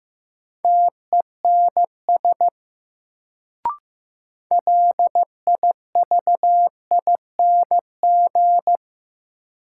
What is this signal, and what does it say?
Morse code, heard as a keyed tone, about 15 wpm, 700 Hz. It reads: TENS LIVING
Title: Sets of 4 Words - New Word Drill - 15wpm